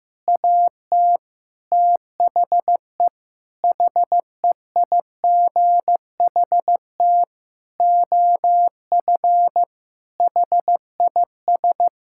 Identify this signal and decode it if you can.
Morse code, heard as a keyed tone, about 15 wpm, 700 Hz. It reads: AT THE HEIGHT OF HIS
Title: Sentences from Top 1600 Words 15wpm